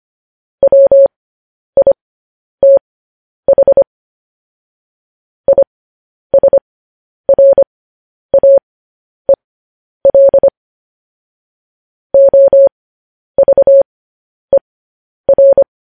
Morse code, heard as a keyed tone, about 25 wpm, spaced as for 10 wpm, 550 Hz. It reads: WITH ISRAEL OVER
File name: podcast / News headlines-1x-25x10